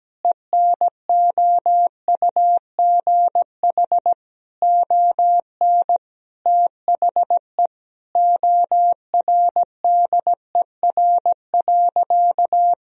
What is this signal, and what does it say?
Morse code, heard as a keyed tone, about 17 wpm, 700 Hz. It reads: ENOUGH ON THE ORDER.
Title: Sentences from Top 1000 Words - Review Part 2 - 17wpm